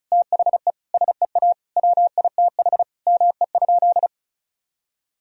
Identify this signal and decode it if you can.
Morse code, heard as a keyed tone, about 34 wpm, 700 Hz. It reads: THE SEA WITH ME?
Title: Sentences from Top 1000 Words - Review Part 2 - 35wpm